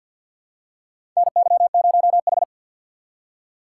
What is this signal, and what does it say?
Morse code, heard as a keyed tone, about 47 wpm, 700 Hz. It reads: NY0H
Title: USA Call Signs - 4 Characters - 3x Character Spacing - 50wpm